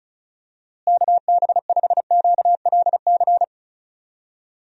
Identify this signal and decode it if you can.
Morse code, heard as a keyed tone, about 34 wpm, 700 Hz. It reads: KB5QLC
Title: USA Call Signs - 6 Characters - Repeated 2x - 35wpm